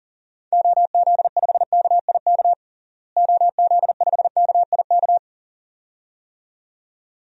Morse code, heard as a keyed tone, about 40 wpm, 700 Hz. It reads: OZ5KIK OZ5KIK
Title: DXCC-100 Call Signs - 6 Characters - Repeated 2x - 40wpm